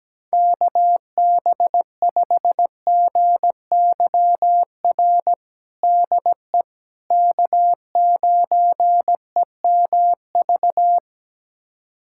Morse code, heard as a keyed tone, about 17 wpm, 700 Hz. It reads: KB5GYR DE K9EMV
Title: USA Callsign DE Callsign 17wpm